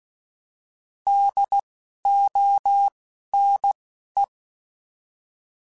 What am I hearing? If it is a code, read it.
Morse code, heard as a keyed tone, about 16 wpm, 800 Hz. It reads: D O N E
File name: Alphabetical / Alphabetical mp3s / Ds @10